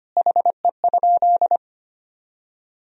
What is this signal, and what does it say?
Morse code, heard as a keyed tone, about 25 wpm, 700 Hz. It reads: HE?
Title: Sentences from Top 300 Words and Abbreviations 25wpm